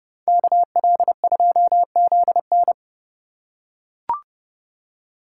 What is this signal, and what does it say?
Morse code, heard as a keyed tone, about 31 wpm, 700 Hz. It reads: KL2ZN